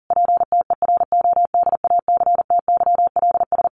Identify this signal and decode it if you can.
Morse code, heard as a keyed tone, about 40 wpm, 700 Hz. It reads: PTERODACTYLS
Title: A Sound Of Thunder 40wpm